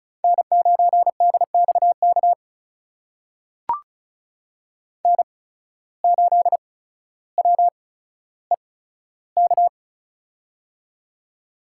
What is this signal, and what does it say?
Morse code, heard as a keyed tone, about 36 wpm, spaced as for 15 wpm, 700 Hz. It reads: N9DXK N8WEK
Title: USA Call Signs - 5 Characters - 8x Character Spacing - 35wpm